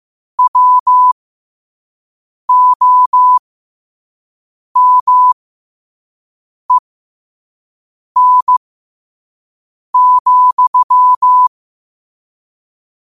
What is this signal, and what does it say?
Morse code, heard as a keyed tone, about 15 wpm, spaced as for 5 wpm, 1000 Hz. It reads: WOMEN,
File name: qotdfiles / 5wpm